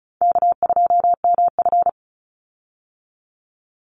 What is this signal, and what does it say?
Morse code, heard as a keyed tone, about 34 wpm, 700 Hz. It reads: K2MF
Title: Easy Call Signs Repeated 3x - 35wpm